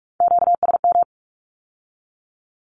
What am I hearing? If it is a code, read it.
Morse code, heard as a keyed tone, about 46 wpm, 700 Hz. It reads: QSM
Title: Q-Codes with Spoken Meaning - 45wpm